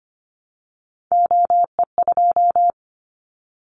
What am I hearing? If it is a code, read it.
Morse code, heard as a keyed tone, about 25 wpm, 700 Hz. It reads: OE2
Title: Callsign Prefixes for Top 25 DXCC - 25wpm